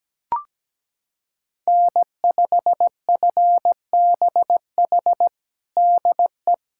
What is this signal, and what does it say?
Morse code, heard as a keyed tone, about 17 wpm, 700 Hz. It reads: N5FBH DE
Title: USA Callsign DE Callsign - 5 Characters - 17wpm